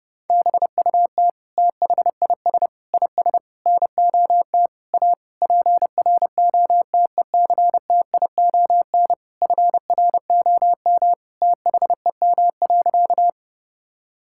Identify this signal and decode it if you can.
Morse code, heard as a keyed tone, about 30 wpm, 700 Hz. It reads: BUT THIS IS NOT A PROTECTION FROM THEM.
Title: Sentences from Top 1600 Words 30wpm